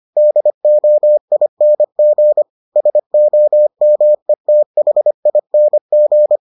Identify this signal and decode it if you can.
Morse code, heard as a keyed tone, about 25 wpm, 600 Hz. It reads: DOING SOMETHING